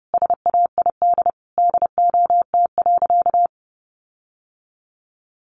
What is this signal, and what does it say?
Morse code, heard as a keyed tone, about 30 wpm, 700 Hz. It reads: SAID DOT.
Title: Sentences from Top 1500 Words 30wpm